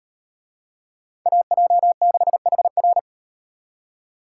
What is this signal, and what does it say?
Morse code, heard as a keyed tone, about 37 wpm, 700 Hz. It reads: AJ6HR